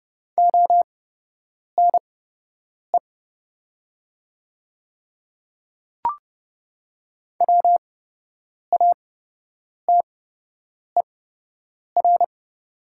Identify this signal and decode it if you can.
Morse code, heard as a keyed tone, about 30 wpm, spaced as for 8 wpm, 700 Hz. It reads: ONE WATER
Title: 8x Character Spacing - Top 100 Words - 30wpm